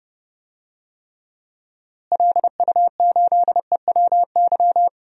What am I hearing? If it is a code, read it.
Morse code, heard as a keyed tone, about 30 wpm, 700 Hz. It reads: LU8EWY
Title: DXCC-100 Call Signs - 6 Characters - Repeated 2x - 30wpm